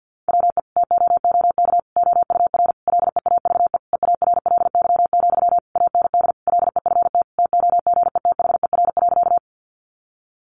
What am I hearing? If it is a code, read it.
Morse code, heard as a keyed tone, about 50 wpm, 700 Hz. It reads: WE TOOK OUR LEAVE EARLY, AND LEFT TOGETHER.
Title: Sentences from Top 400 Words 50wpm